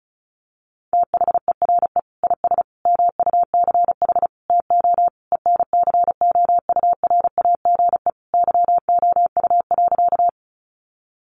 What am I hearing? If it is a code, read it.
Morse code, heard as a keyed tone, about 34 wpm, 700 Hz. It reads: THERE IS MUCH TO ENCOURAGE YOU.
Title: Sentences from Top 1400 Words 35wpm